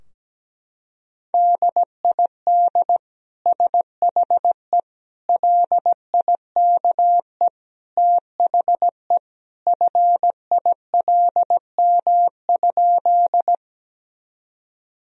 Morse code, heard as a keyed tone, about 17 wpm, 700 Hz. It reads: DID SHE LIKE THE FILM?
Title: Sentences from Top 1000 Words - Review Part 3 - 17wpm